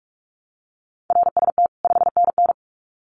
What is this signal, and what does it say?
Morse code, heard as a keyed tone, about 44 wpm, 700 Hz. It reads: RST 5NN
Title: Common QSO Phrases 45wpm